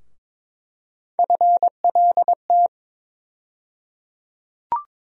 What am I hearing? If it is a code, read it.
Morse code, heard as a keyed tone, about 22 wpm, 700 Hz. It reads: FLT
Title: USA Call Sign Postfixes - 3 Letters 22wpm